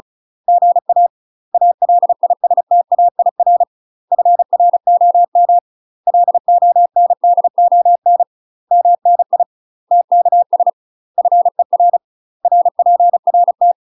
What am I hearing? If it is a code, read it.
Morse code, heard as a keyed tone, about 33 wpm, 700 Hz. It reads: GA ALISTAIR FROM LONDON MNI TKS FER RPRT
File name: UK MASTER QSOs TRAINING / 35WPM / UK MASTER QSOs 0002